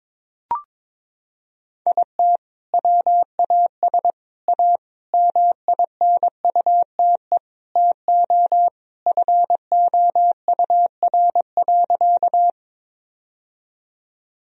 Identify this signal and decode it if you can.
Morse code, heard as a keyed tone, about 22 wpm, 700 Hz. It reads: IT WAS A MINUTE TO FOUR.
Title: Sentences from Top 1000 Words - Review Part 1 - 22wpm